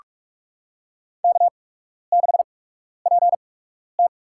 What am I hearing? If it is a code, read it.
Morse code, heard as a keyed tone, about 46 wpm, spaced as for 12 wpm, 700 Hz. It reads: K6PT
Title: USA Call Signs - 4 Characters - 8x Character Spacing - 45wpm